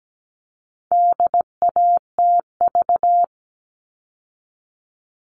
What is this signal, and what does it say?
Morse code, heard as a keyed tone, about 17 wpm, 700 Hz. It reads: DATV
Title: All Cut Numbers - 4-Digit 17wpm